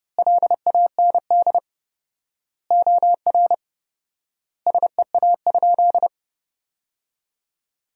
Morse code, heard as a keyed tone, about 30 wpm, 700 Hz. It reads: LAND OR SEA?
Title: 4x Word-Spacing - Sentences from Top 500 Words - 30wpm